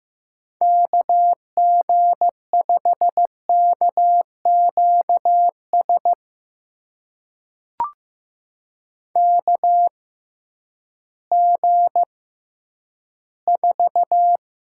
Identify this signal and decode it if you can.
Morse code, heard as a keyed tone, about 15 wpm, spaced as for 10 wpm, 700 Hz. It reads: KG5KQS KG4
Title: USA Call Signs - 6 Characters - 6x Character Spacing - 15wpm